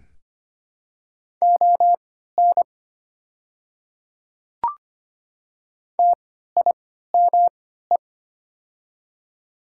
Morse code, heard as a keyed tone, about 25 wpm, spaced as for 14 wpm, 700 Hz. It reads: ON TIME
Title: 3x Character Spacing - Top 100 Words - 25wpm